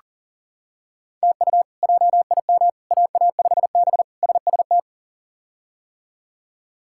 Morse code, heard as a keyed tone, about 40 wpm, 700 Hz. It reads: TU JIM AA5B SST
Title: K1USN SST QSO Practice - 40wpm